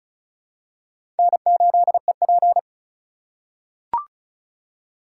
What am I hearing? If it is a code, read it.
Morse code, heard as a keyed tone, about 36 wpm, 700 Hz. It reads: N8EP
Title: USA Call Signs - 4 Characters - 4x Character Spacing - 35wpm